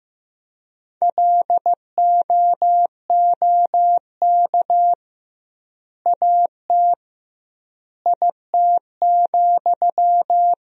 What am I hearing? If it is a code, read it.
Morse code, heard as a keyed tone, about 15 wpm, 700 Hz. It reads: LOOK AT IT,
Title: 2x Word-Spacing - Sentences from Top 200 Words - 15wpm